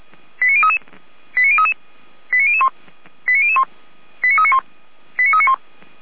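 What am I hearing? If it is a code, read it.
Five-tone selective call (ZVEI), heard as quick runs of five tones, tones of about 80 ms. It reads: 89033 89033 89001 89001 89391 89391
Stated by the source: ZVEI five-tone call 89502 received by SDR, from a Feuerwehr Orlamünde archive.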